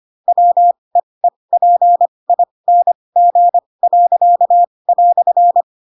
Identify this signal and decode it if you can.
Morse code, heard as a keyed tone, about 25 wpm, spaced as for 20 wpm, 700 Hz. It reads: WEEPING."
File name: Pinocchio / Pinocchio ch7 15 25 ik1hge